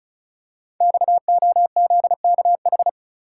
Morse code, heard as a keyed tone, about 36 wpm, 700 Hz. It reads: XOZKH
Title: Code Groups - 5 Letter:Numbers - 35wpm